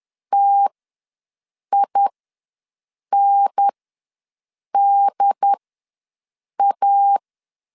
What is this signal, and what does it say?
Morse code, heard as a keyed tone, about 10 wpm, spaced as for 6 wpm, 800 Hz. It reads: TINDA
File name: LEZIONI / 5 LBVC / 5 L ws 10wpm 3636